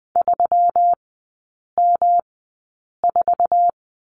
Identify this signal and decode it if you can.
Morse code, heard as a keyed tone, about 20 wpm, spaced as for 8 wpm, 700 Hz. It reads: 3M4
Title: 121 Lesson 20e - Number 3 - Three Characters 20wpm